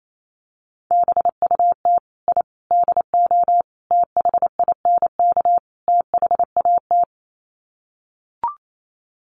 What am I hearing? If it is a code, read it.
Morse code, heard as a keyed tone, about 28 wpm, 700 Hz. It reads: BUT I DO THINK THAT
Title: Sets of 5 Words Encore2 28wpm